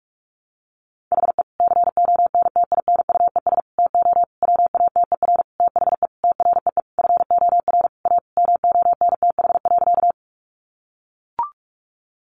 Morse code, heard as a keyed tone, about 44 wpm, 700 Hz. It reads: HE CONTINUES TO WATER THE TREE FOR A MONTH.
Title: Easy Phrases 45wpm